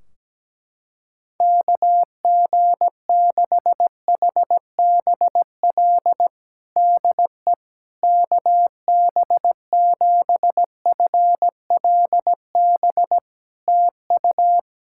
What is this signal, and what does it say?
Morse code, heard as a keyed tone, about 17 wpm, 700 Hz. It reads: KG6HBL DE KB7FLB TU
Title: Practice QSO - Exchange 4 - Ending - 17wpm